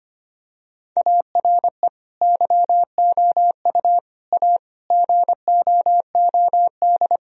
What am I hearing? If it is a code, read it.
Morse code, heard as a keyed tone, about 25 wpm, 700 Hz. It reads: ARE YOU A GOOD
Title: Sentences from Top 1000 Words - Review Part 4 - 25wpm